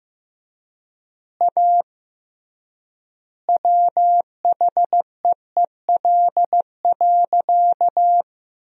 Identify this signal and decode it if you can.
Morse code, heard as a keyed tone, about 15 wpm, 700 Hz. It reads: A WHEEL.
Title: 3x Word-Spacing - Sentences from Top 500 Words - 15wpm